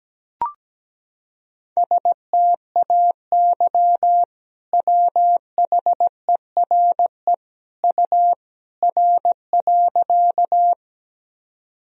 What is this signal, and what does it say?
Morse code, heard as a keyed tone, about 17 wpm, 700 Hz. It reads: STAY WHERE U R.
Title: Sentences from Top 500 Words and Abbreviations 17wpm